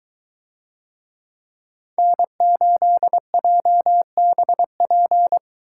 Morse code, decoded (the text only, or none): N8JBP